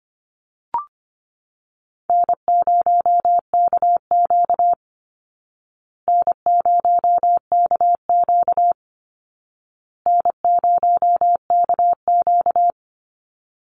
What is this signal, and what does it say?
Morse code, heard as a keyed tone, about 25 wpm, 700 Hz. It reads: N0KQ N0KQ N0KQ
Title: Easy Call Signs Repeated 3x - 25wpm